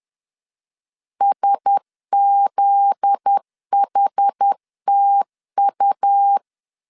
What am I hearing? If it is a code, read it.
Morse code, heard as a keyed tone, about 10 wpm, 800 Hz. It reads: SZHTU
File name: LEZIONI / 7 GF9Z / 7 GF9Z 10wpm 2611